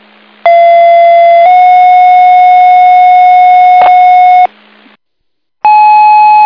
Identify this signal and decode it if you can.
Two-tone page, heard as a one-second tone, then a three-second tone, about 680 then 730 Hz, with noise. It is heard from 0.4 to 4.5 s.